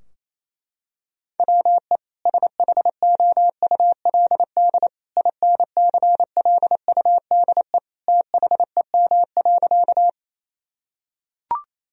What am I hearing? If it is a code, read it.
Morse code, heard as a keyed tone, about 28 wpm, 700 Hz. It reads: WE SHOULD INCLUDE THEM.
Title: Sentences from Top 1000 Words - Review Part 3 - 28wpm